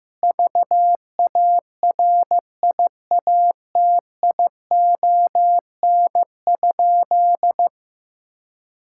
Morse code, heard as a keyed tone, about 16 wpm, 700 Hz. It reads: VARIATION?
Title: Sentences from Top 1600 Words 15wpm